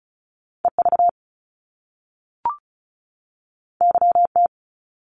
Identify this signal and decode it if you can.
Morse code, heard as a keyed tone, about 36 wpm, 700 Hz. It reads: EV YT
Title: Top Two Letter Combinations in Words 35wpm